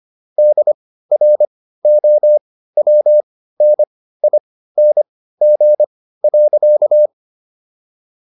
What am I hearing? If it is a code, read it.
Morse code, heard as a keyed tone, about 25 wpm, spaced as for 15 wpm, 600 Hz. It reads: DROWNING.